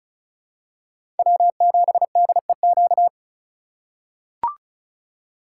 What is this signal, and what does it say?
Morse code, heard as a keyed tone, about 36 wpm, 700 Hz. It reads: W7DEQ
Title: USA Call Signs - 5 Characters - 2x Character Spacing - 35wpm